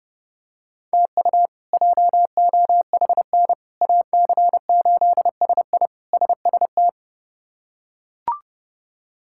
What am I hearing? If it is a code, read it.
Morse code, heard as a keyed tone, about 32 wpm, 700 Hz. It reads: TU JOHN AC8SI SST
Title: K1USN SST QSO Practice - 30wpm